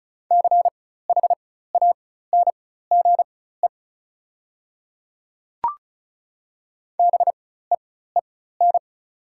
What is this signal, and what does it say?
Morse code, heard as a keyed tone, about 36 wpm, spaced as for 16 wpm, 700 Hz. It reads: CHANGE BEEN
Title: 4x Character Spacing - Top 200 Words - 35wpm